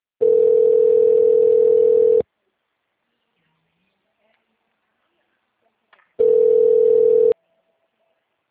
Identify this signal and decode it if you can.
Ringback tone (North America), 440 + 480 Hz.
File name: out- -2016- - - .126